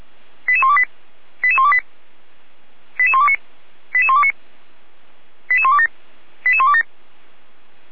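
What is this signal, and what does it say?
Five-tone selective call (ZVEI), heard as quick runs of five tones, tones of about 70 ms. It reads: 88128 88128 88129 88129 88127 88127
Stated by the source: ZVEI five-tone selective call received by SDR.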